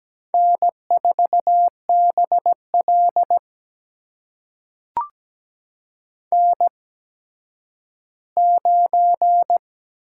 Morse code, heard as a keyed tone, about 17 wpm, spaced as for 10 wpm, 700 Hz. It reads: N4BL N9
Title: USA Call Signs - 4 Characters - 8x Character Spacing - 17wpm